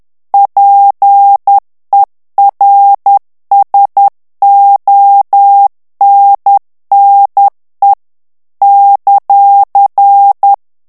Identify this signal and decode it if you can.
Morse code, heard as a keyed tone, about 10 wpm, 800 Hz. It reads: PERSONNE ;
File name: FRANCAIS TEXTE CLAIR 2 VITESSE 10 MOTS MINUTES